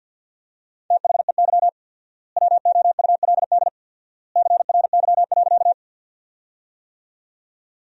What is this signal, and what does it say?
Morse code, heard as a keyed tone, about 50 wpm, 700 Hz. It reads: THEY WOULD CRY.
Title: 4x Word-Spacing - Sentences from Top 500 Words - 50wpm